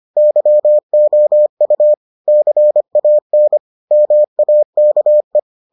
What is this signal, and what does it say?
Morse code, heard as a keyed tone, about 25 wpm, 600 Hz. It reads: YOU CAN MAKE